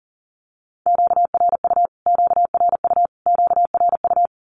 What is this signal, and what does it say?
Morse code, heard as a keyed tone, about 40 wpm, 700 Hz. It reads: QRU QRU QRU
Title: Q-Codes Repeated 3x 40wpm